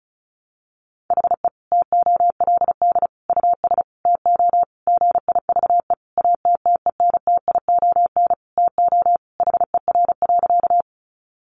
Morse code, heard as a keyed tone, about 34 wpm, 700 Hz. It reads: HE TOLD US TO GIVE ATTENTION TO HER.
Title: Sentences from Top 1000 Words 35wpm